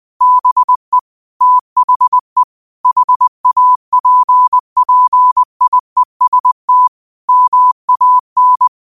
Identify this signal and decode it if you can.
Morse code, heard as a keyed tone, about 20 wpm, 1000 Hz. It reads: BE THE HAPPIEST MAN
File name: qotdfiles / 20wpm